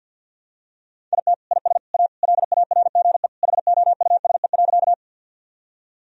Morse code, heard as a keyed tone, about 47 wpm, 700 Hz. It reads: IT IS A LARGE HOUSE.
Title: Sentences from Top 1000 Words - Review Part 1 - 50wpm